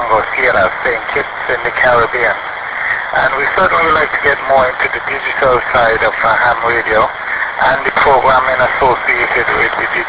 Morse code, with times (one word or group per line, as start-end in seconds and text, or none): none